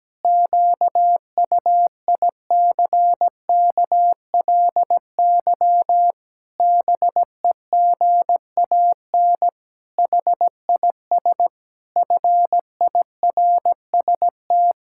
Morse code, heard as a keyed tone, about 17 wpm, 700 Hz. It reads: QUICKLY BEGAN HIS FIRST
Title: Pangram Challenge v2 - 17wpm